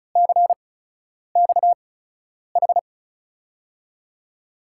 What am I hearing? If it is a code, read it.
Morse code, heard as a keyed tone, about 36 wpm, spaced as for 10 wpm, 700 Hz. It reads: CXH
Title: USA Call Signs - 8x Character Spacing - 35wpm